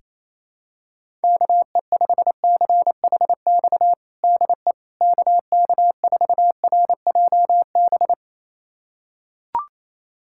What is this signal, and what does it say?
Morse code, heard as a keyed tone, about 28 wpm, 700 Hz. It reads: KE5CHX DE KK4RJB
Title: USA Callsign DE Callsign - 6 Characters - 28wpm